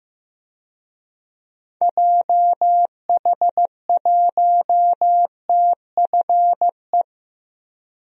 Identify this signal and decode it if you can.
Morse code, heard as a keyed tone, about 15 wpm, 700 Hz. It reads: JH1TFE
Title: DXCC-100 Call Signs - 6 Characters - Repeated 2x - 15wpm